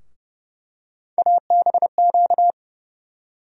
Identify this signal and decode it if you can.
Morse code, heard as a keyed tone, about 30 wpm, 700 Hz. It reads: ABQ
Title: USA Call Sign Postfixes - 3 Letters 30wpm